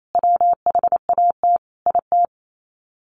Morse code, heard as a keyed tone, about 28 wpm, 700 Hz. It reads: WHAT IT
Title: Sets of 5 Words 28wpm